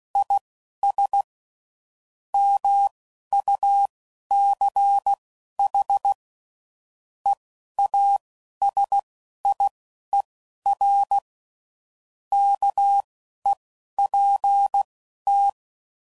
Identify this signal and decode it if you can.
Morse code, heard as a keyed tone, about 16 wpm, spaced as for 11 wpm, 800 Hz. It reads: IS MUCH EASIER KEPT